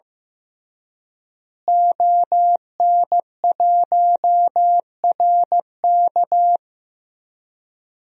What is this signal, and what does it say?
Morse code, heard as a keyed tone, about 15 wpm, 700 Hz. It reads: ON1RK
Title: DXCC-100 Call Signs - 5 Characters - Repeated 3x - 15wpm